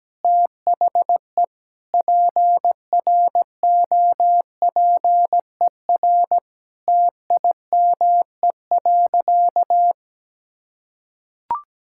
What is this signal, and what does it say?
Morse code, heard as a keyed tone, about 17 wpm, 700 Hz. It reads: THE PROPER TIME.
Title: Sentences from Top 1000 Words - Review Part 5 - 17wpm